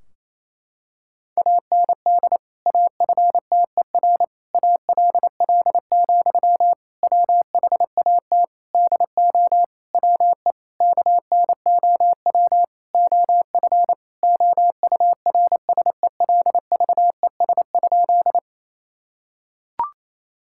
Morse code, heard as a keyed tone, about 27 wpm, 700 Hz. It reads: AND AFTER ALL, WHAT DO WE KNOW OF OURSELVES?
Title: Sentences from Top 1600 Words 28wpm